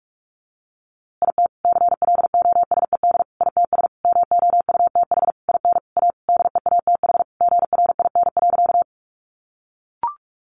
Morse code, heard as a keyed tone, about 44 wpm, 700 Hz. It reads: IT CLOSED ITS MOUTH IN A DEATH GRIN.
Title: A Sound Of Thunder 45wpm